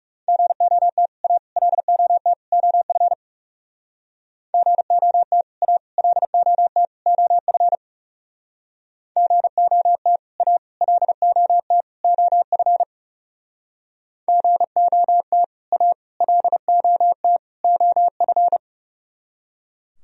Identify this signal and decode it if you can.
Morse code, heard as a keyed tone, about 39 wpm, 700 Hz. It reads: GOT A LOT OF GOT A LOT OF GOT A LOT OF GOT A LOT OF